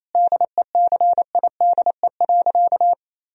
Morse code, heard as a keyed tone, about 28 wpm, 700 Hz. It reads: DECIDE.
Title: Sentences from Top 1000 Words - Review Part 3 - 28wpm